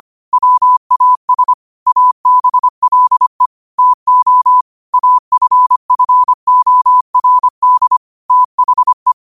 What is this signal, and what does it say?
Morse code, heard as a keyed tone, about 25 wpm, 1000 Hz. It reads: WAS ABLE TO AFFORD THE
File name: qotdfiles / 25wpm